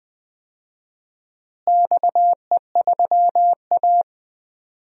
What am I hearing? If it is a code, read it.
Morse code, heard as a keyed tone, about 20 wpm, 700 Hz. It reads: XE3A
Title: DXCC-100 Call Signs - 4 Characters - Repeated 2x - 20wpm